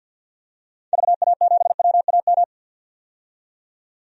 Morse code, heard as a keyed tone, about 50 wpm, 700 Hz. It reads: VA7WAM
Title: DXCC-100 Call Signs - 6 Characters - 50wpm